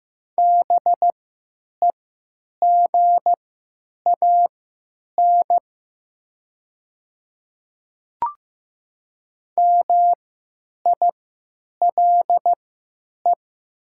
Morse code, heard as a keyed tone, about 15 wpm, spaced as for 9 wpm, 700 Hz. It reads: BEGAN MILE
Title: 3x Character Spacing - Top 300 Words - 15wpm